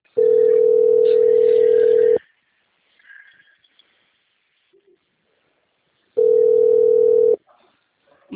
Ringback tone (North America), 440 + 480 Hz.